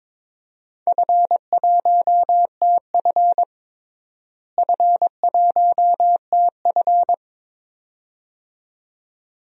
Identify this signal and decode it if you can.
Morse code, heard as a keyed tone, about 22 wpm, 700 Hz. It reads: F1TF F1TF
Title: DXCC-100 Call Signs - 4 Characters - Repeated 2x - 22wpm